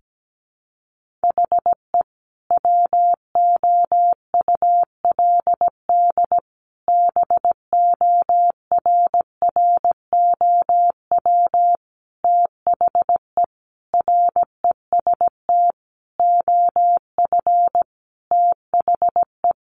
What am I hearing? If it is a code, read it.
Morse code, heard as a keyed tone, about 17 wpm, 700 Hz. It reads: HE WOULD BORROW THE REST OF THE
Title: Sentences from Top 1500 Words 17wpm